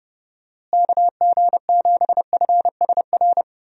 Morse code, heard as a keyed tone, about 30 wpm, 700 Hz. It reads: KG7FSR
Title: USA Call Signs - 6 Characters - Repeated 2x - 30wpm